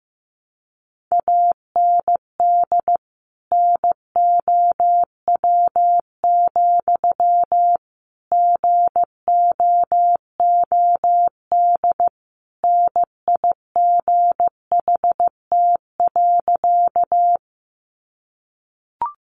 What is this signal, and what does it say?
Morse code, heard as a keyed tone, about 15 wpm, 700 Hz. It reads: AND NOW, GOOD NIGHT.
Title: Sentences from Top 300 Words 15wpm